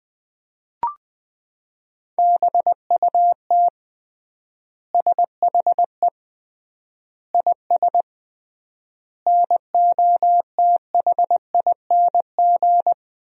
Morse code, heard as a keyed tone, about 20 wpm, 700 Hz. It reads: BUT SHE IS NOTHING
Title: 3x Word-Spacing - Sentences from Top 500 Words - 20wpm